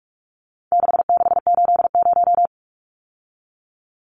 Morse code, heard as a keyed tone, about 44 wpm, 700 Hz. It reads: 6680
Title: Numbers - 4-Digit 45wpm